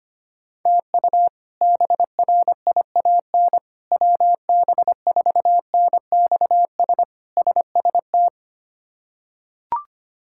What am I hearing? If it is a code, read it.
Morse code, heard as a keyed tone, about 25 wpm, 700 Hz. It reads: TU BRIAN WB4NXS SST
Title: K1USN SST QSO Practice - 25wpm